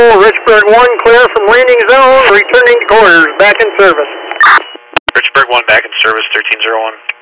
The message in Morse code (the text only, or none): none